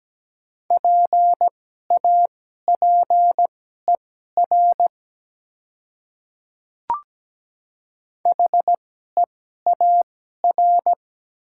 Morse code, heard as a keyed tone, about 17 wpm, spaced as for 12 wpm, 700 Hz. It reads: PAPER HEAR
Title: 2x Character Spacing - Top 300 Words - 17wpm